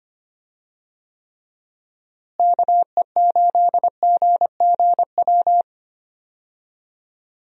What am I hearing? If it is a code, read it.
Morse code, heard as a keyed tone, about 25 wpm, 700 Hz. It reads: KE8GGW
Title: USA Call Signs - 6 Characters - Repeated 3x - 25wpm